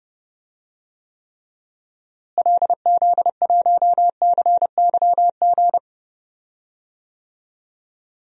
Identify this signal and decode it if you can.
Morse code, heard as a keyed tone, about 32 wpm, 700 Hz. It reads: LZ1CYG
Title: DXCC-100 Call Signs - 6 Characters - Repeated 2x - 30wpm